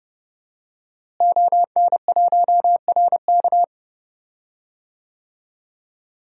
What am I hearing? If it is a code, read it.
Morse code, heard as a keyed tone, about 30 wpm, 700 Hz. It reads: ON1RK
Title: DXCC-100 Call Signs - 5 Characters - Repeated 3x - 30wpm